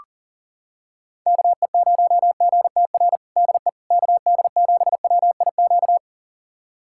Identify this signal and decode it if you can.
Morse code, heard as a keyed tone, about 40 wpm, 700 Hz. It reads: KE0GTR DE KD7WIQ
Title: USA Callsign DE Callsign - 6 Characters - 40wpm